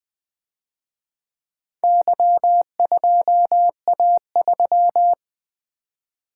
Morse code, heard as a keyed tone, about 25 wpm, 700 Hz. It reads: Y2A3
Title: Speed-Racing - Four Letter-Number Combinations - 20wpm